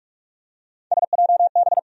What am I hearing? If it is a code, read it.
Morse code, heard as a keyed tone, about 46 wpm, 700 Hz. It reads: SJB
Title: USA Call Sign Postfixes - 3 Letters 45wpm